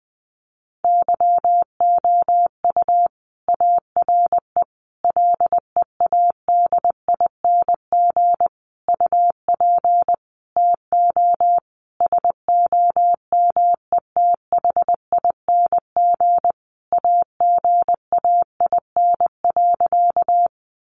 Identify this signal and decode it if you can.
Morse code, heard as a keyed tone, about 20 wpm, 700 Hz. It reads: YOU ARE LEADING UP TO SOMETHING AGAIN.
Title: Sentences from Top 1100 Words 20wpm